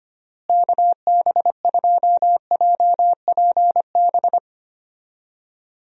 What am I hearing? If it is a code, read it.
Morse code, heard as a keyed tone, about 25 wpm, 700 Hz. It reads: KB2JPB
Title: USA Call Signs - 6 Characters - Repeated 3x - 25wpm